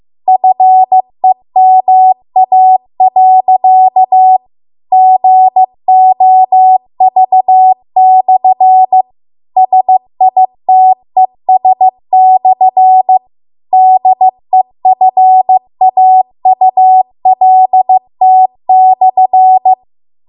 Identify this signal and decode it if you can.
Morse code, heard as a keyed tone, about 15 wpm, 750 Hz. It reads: FEMA.GOV/SITES/DEFAULT/